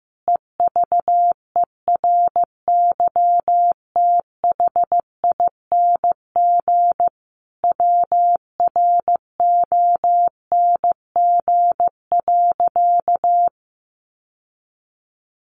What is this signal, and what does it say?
Morse code, heard as a keyed tone, about 15 wpm, 700 Hz. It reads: EVERYTHING WRONG.
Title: Sentences from Top 900 Words 15wpm